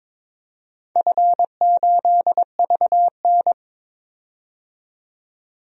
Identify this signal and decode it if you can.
Morse code, heard as a keyed tone, about 23 wpm, 700 Hz. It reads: F8VN
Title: DXCC-100 Call Signs - 4 Characters - 22wpm